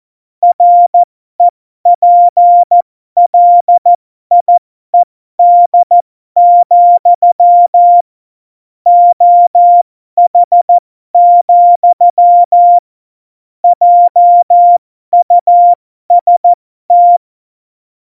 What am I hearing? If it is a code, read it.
Morse code, heard as a keyed tone, about 14 wpm, spaced as for 12 wpm, 700 Hz. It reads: REPLIED, OH, JUST
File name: Morsekurs / Witze